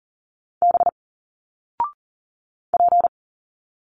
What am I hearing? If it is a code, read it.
Morse code, heard as a keyed tone, about 39 wpm, 700 Hz. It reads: B P
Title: Single Letters 40wpm